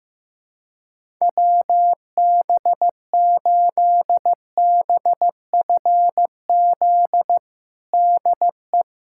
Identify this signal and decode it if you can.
Morse code, heard as a keyed tone, about 16 wpm, 700 Hz. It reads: WB8BFZ DE
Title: USA Callsign DE Callsign - 6 Characters - 15wpm